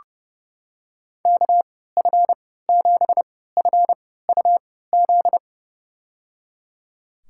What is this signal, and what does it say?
Morse code, heard as a keyed tone, about 30 wpm, spaced as for 17 wpm, 700 Hz. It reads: KF7FUZ
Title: USA Call Signs - 6 Characters - 3x Character Spacing - 30wpm